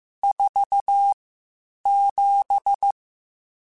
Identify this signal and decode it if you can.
Morse code, heard as a keyed tone, about 15 wpm, spaced as for 8 wpm, 800 Hz. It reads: 47
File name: Spaced Characters / Spaced Characters-mp3s / Weather 5 5 sp @13